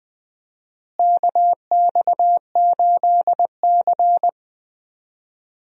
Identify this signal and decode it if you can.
Morse code, heard as a keyed tone, about 20 wpm, 700 Hz. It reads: KX8C
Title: USA Call Signs - 4 Characters - Repeated 2x - 20wpm